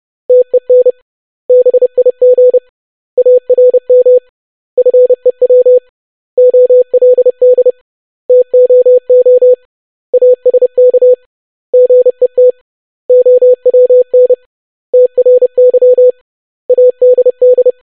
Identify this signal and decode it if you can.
Morse code, heard as a keyed tone, about 30 wpm, 500 Hz. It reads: TEN BIG ARM FEW OLD TOO ASK GET OWN TRY ADD
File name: basic500-25wpm